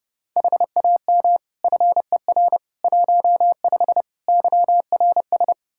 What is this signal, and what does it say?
Morse code, heard as a keyed tone, about 32 wpm, 700 Hz. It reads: HAM FER 15 YRS
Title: K1USN SST QSO Practice 30wpm